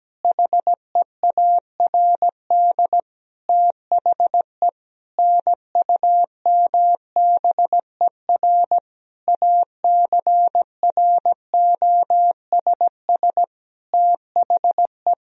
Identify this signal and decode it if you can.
Morse code, heard as a keyed tone, about 17 wpm, 700 Hz. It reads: HEARD THE NUMBER ACROSS THE
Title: Sentences from Top 1000 Words - Review Part 1 - 17wpm